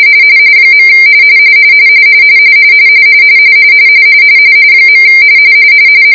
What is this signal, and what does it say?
Two-tone teleprinter signal, about 170 Hz shift.